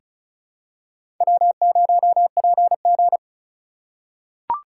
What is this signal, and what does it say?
Morse code, heard as a keyed tone, about 36 wpm, 700 Hz. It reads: W0PG
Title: USA Call Signs - 4 Characters - 4x Character Spacing - 35wpm